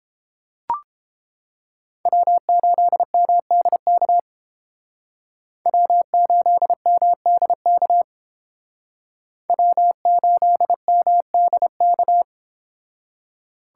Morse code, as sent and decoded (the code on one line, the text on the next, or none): .-- ---.. -- -.. -.- / .-- ---.. -- -.. -.- / .-- ---.. -- -.. -.-
W8MDK W8MDK W8MDK